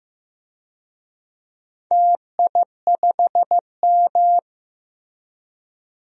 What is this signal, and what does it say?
Morse code, heard as a keyed tone, about 15 wpm, 700 Hz. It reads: TI5M
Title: DXCC-100 Call Signs - 4 Characters - Repeated 2x - 15wpm